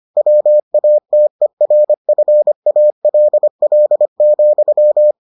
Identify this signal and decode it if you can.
Morse code, heard as a keyed tone, about 25 wpm, 600 Hz. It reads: WATERFALL,